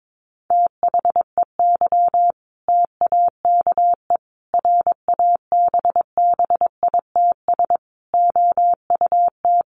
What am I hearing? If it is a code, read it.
Morse code, heard as a keyed tone, about 22 wpm, 700 Hz. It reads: THEY TAKE RABBITS OUT
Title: Easy Phrases 22wpm